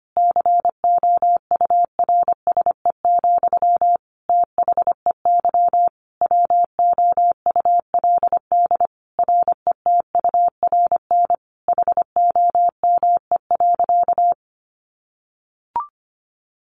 Morse code, heard as a keyed tone, about 25 wpm, 700 Hz. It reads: COURSE, THEY WOULD RETURN HOME.
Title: Sentences from Top 1400 Words 25wpm